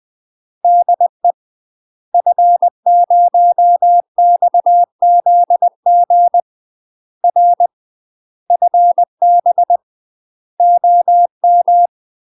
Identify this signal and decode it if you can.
Morse code, heard as a keyed tone, about 20 wpm, 700 Hz. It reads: DE F0XZG R FB OM
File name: UK MASTER QSOs TRAINING / 20WPM / UK MASTER QSOs 0022